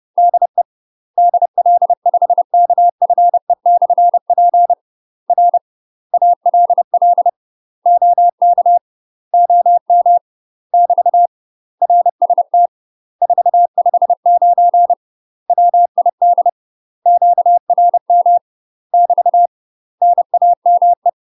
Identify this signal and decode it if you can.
Morse code, heard as a keyed tone, about 30 wpm, 700 Hz. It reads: DE DL5KFE/P R ALL OK OM <BT> RST 459 WID QRM <BT> NAME
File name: UK MASTER QSOs TRAINING/30WPM